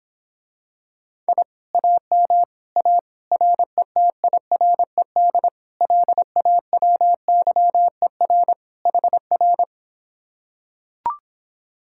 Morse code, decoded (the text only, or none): I AM A RETIRED LAWYER HR